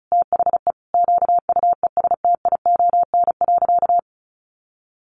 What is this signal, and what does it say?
Morse code, heard as a keyed tone, about 34 wpm, 700 Hz. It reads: THE QUESTION.
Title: Sentences from Top 1400 Words 35wpm